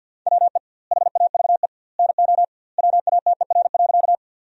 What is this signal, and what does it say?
Morse code, heard as a keyed tone, about 50 wpm, 700 Hz. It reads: WE HAVE NO WATER.
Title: 2x Word-Spacing - Sentences from Top 100 Words - 50wpm